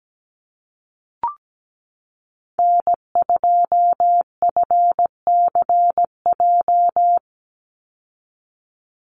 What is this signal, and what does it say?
Morse code, heard as a keyed tone, about 17 wpm, 700 Hz. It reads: N2FCJ
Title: USA Call Signs 17wpm